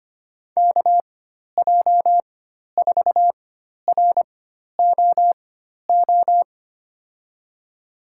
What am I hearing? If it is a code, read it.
Morse code, heard as a keyed tone, about 25 wpm, spaced as for 12 wpm, 700 Hz. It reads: KJ4ROO
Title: USA Call Signs - 6 Characters - 4x Character Spacing - 25wpm